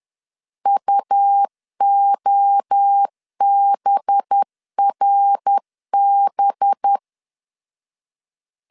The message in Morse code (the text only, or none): UOBRB